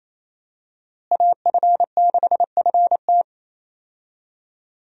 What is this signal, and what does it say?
Morse code, heard as a keyed tone, about 28 wpm, 700 Hz. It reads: AF6FT
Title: USA Call Signs - 5 Characters - Repeated 3x - 28wpm